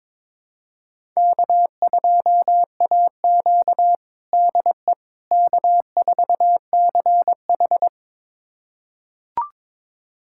Morse code, heard as a keyed tone, about 21 wpm, 700 Hz. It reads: K2AQ DE K4CH
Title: USA Callsign DE Callsign - 4 Characters - 22wpm